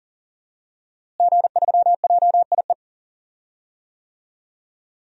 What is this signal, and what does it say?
Morse code, heard as a keyed tone, about 40 wpm, 700 Hz. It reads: G3JIE
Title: DXCC-100 Call Signs - 5 Characters - Repeated 3x - 40wpm